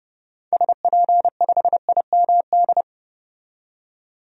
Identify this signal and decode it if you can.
Morse code, heard as a keyed tone, about 30 wpm, 700 Hz. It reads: SP5IMD
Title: DXCC-100 Call Signs - 6 Characters - 30wpm